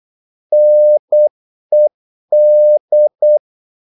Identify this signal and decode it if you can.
Morse code, heard as a keyed tone, about 8 wpm, 600 Hz. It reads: NED
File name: Korte ord DK 8wpm